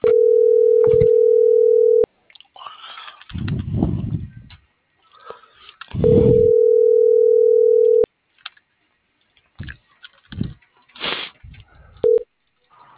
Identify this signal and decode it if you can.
Ringback tone (North America), 440 + 480 Hz.
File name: CENTRAL MABE / out luis.palacios